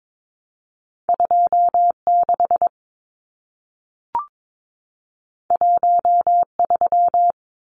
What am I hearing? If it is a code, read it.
Morse code, heard as a keyed tone, about 22 wpm, 700 Hz. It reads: 26 13
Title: Numbers - 2-Digit 22wpm